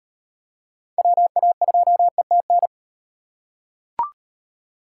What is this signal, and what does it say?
Morse code, heard as a keyed tone, about 35 wpm, 700 Hz. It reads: WA2ETN